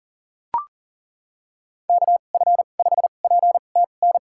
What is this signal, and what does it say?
Morse code, heard as a keyed tone, about 40 wpm, spaced as for 29 wpm, 700 Hz. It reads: KF5PTN
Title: USA Call Signs - 6 Characters - 2x Character Spacing - 40wpm